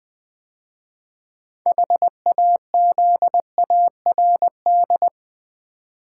Morse code, heard as a keyed tone, about 20 wpm, 690 Hz. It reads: HAZARD
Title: 6 Letter Words - First 3 Letters Given - 20wpm